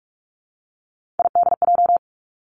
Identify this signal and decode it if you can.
Morse code, heard as a keyed tone, about 46 wpm, 700 Hz. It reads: IDJ
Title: Three Challenging Letter-Number Combinations 45wpm